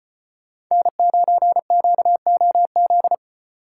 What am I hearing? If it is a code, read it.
Morse code, heard as a keyed tone, about 32 wpm, 700 Hz. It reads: N9QOZ